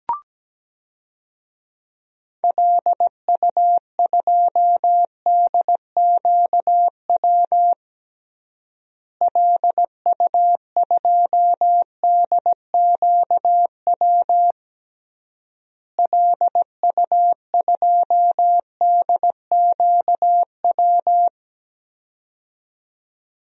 Morse code, heard as a keyed tone, about 17 wpm, 700 Hz. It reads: LU2DQW LU2DQW LU2DQW
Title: DXCC-100 Call Signs - 6 Characters - Repeated 3x - 17wpm